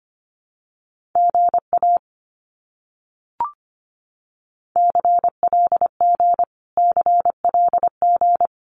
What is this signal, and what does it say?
Morse code, heard as a keyed tone, about 25 wpm, 700 Hz. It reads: GA CLG CLG
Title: Abbreviations Repeated 3x - 25wpm